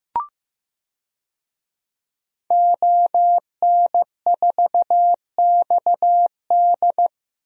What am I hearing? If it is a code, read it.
Morse code, heard as a keyed tone, about 15 wpm, 700 Hz. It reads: ON4XD
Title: DXCC-100 Call Signs - 5 Characters - 15wpm